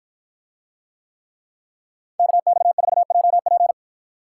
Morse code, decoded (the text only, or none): KX4JP